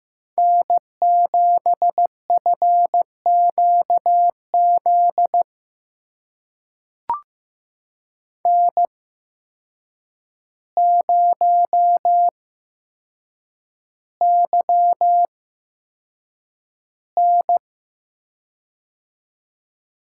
Morse code, heard as a keyed tone, about 15 wpm, spaced as for 7 wpm, 700 Hz. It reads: N7FQZ N0YN
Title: USA Call Signs - 8x Character Spacing - 15wpm